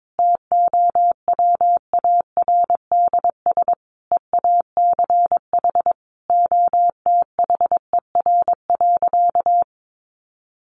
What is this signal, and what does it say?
Morse code, heard as a keyed tone, about 22 wpm, 700 Hz. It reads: TOWARDS EACH OTHER.
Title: Sentences from Top 1200 Words 22wpm